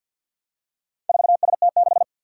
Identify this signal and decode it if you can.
Morse code, heard as a keyed tone, about 51 wpm, 700 Hz. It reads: 4ST6
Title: Four Challenging Letter-Number Combinations 50wpm